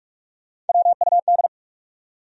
Morse code, heard as a keyed tone, about 60 wpm, spaced as for 53 wpm, 700 Hz. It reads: WUD
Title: Speed-Racing Abbreviations - 45wpm